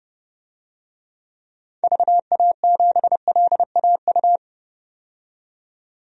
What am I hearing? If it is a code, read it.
Morse code, heard as a keyed tone, about 32 wpm, 700 Hz. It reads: VA7LAU
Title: DXCC-100 Call Signs - 6 Characters - 30wpm